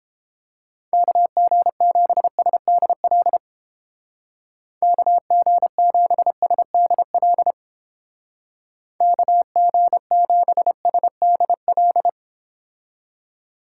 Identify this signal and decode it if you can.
Morse code, heard as a keyed tone, about 29 wpm, 700 Hz. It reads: KG7SDL KG7SDL KG7SDL